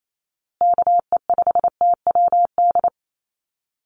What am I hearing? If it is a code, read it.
Morse code, heard as a keyed tone, about 28 wpm, 700 Hz. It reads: KE5TWD
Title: USA Call Signs Repeated 3x 28wpm